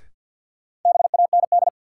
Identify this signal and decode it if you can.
Morse code, heard as a keyed tone, about 53 wpm, 700 Hz. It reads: BAND
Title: Sentences from Top 1600 Words - New Word Drill - 50wpm